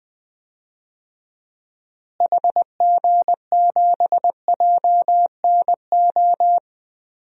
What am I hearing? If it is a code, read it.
Morse code, heard as a keyed tone, about 20 wpm, 700 Hz. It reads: HG7JNO
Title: DXCC-100 Call Signs - 6 Characters - Repeated 2x - 20wpm